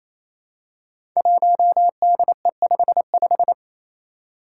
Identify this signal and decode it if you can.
Morse code, heard as a keyed tone, about 27 wpm, 700 Hz. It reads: 1DE55
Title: Five Challenging Letter-Number Combinations 28wpm